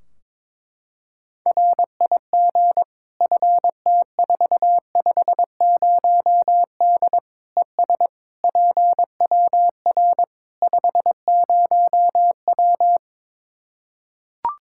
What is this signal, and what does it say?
Morse code, heard as a keyed tone, about 22 wpm, 700 Hz. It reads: RIG FT450D ES PWR 50W
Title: K1USN SST QSO Practice 22wpm